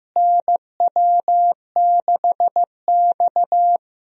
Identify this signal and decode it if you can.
Morse code, heard as a keyed tone, about 15 wpm, 700 Hz. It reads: NW6X
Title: USA Call Signs - 4 Characters - Repeated 2x - 15wpm